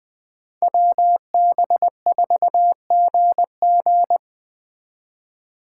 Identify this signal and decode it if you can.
Morse code, heard as a keyed tone, about 24 wpm, 700 Hz. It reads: WB4GG
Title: Speed Racing - USA 2x2 Call Signs 20wpm